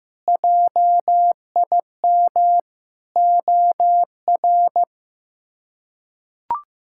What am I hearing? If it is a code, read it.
Morse code, heard as a keyed tone, about 16 wpm, 700 Hz. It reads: JIM OR
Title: K1USN SST QSO Practice - 15wpm